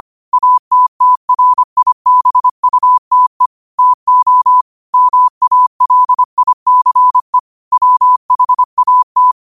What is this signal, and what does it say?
Morse code, heard as a keyed tone, about 25 wpm, 1000 Hz. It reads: ATTRIBUTE TO MALICE WHAT